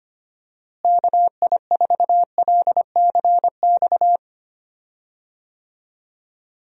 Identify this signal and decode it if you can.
Morse code, heard as a keyed tone, about 25 wpm, 700 Hz. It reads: KI4LCX
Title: USA Call Signs - 6 Characters - Repeated 2x - 25wpm